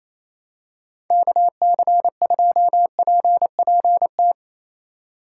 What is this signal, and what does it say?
Morse code, heard as a keyed tone, about 28 wpm, 700 Hz. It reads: KC2PPT
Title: USA Call Signs - 6 Characters - 6x Character Spacing - 28wpm